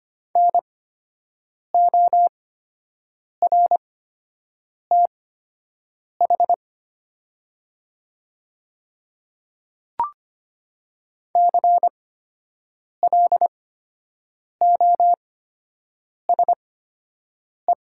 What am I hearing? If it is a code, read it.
Morse code, heard as a keyed tone, about 25 wpm, spaced as for 7 wpm, 700 Hz. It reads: NORTH CLOSE
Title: 8x Character Spacing - Top 500 Words - 25wpm